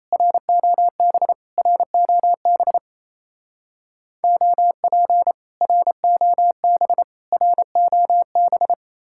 Morse code, heard as a keyed tone, about 34 wpm, 700 Hz. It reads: ROB ROB OP ROB ROB
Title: Speed-Racing Common QSO Phrases - 28wpm